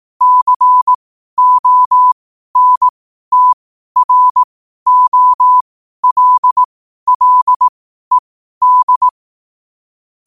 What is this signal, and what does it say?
Morse code, heard as a keyed tone, about 18 wpm, spaced as for 12 wpm, 1000 Hz. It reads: CONTROLLED
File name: qotdfiles / 12wpm